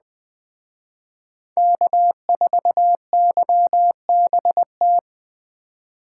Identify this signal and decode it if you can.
Morse code, heard as a keyed tone, about 20 wpm, 700 Hz. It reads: K4YBT
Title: Speed Racing - USA 1x3 Call Signs 17wpm